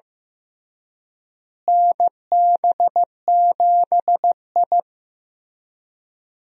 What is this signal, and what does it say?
Morse code, heard as a keyed tone, about 15 wpm, 700 Hz. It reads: NB7I
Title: USA Call Signs - 4 Characters - Repeated 3x - 15wpm